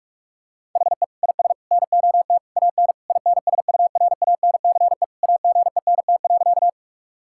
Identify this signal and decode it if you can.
Morse code, heard as a keyed tone, about 46 wpm, 700 Hz. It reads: HE IS NOT AN INSURANCE AGENT.
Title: Sentences from Top 1600 Words 45wpm